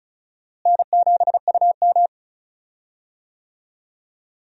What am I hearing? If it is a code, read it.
Morse code, heard as a keyed tone, about 36 wpm, 700 Hz. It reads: N7UM
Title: USA Call Signs - 4 Characters - Repeated 3x - 35wpm